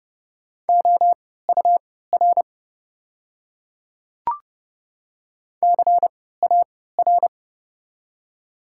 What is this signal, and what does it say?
Morse code, heard as a keyed tone, about 30 wpm, spaced as for 17 wpm, 700 Hz. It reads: OUR CAR
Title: 3x Character Spacing - Top 300 Words - 30wpm